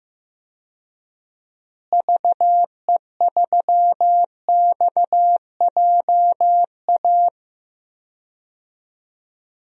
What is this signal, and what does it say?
Morse code, heard as a keyed tone, about 15 wpm, 700 Hz. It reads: VE3XJA
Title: DXCC-100 Call Signs - 6 Characters - Repeated 2x - 15wpm